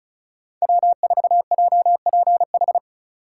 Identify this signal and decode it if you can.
Morse code, heard as a keyed tone, about 36 wpm, 700 Hz. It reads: W4JPH
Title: K1USN SST QSO Practice - 35wpm